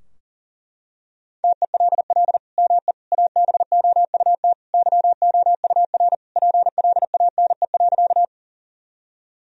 Morse code, heard as a keyed tone, about 40 wpm, 700 Hz. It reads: TELL ME ABOUT YOUR PLANE.
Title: Sentences from Top 1000 Words - Review Part 3 - 40wpm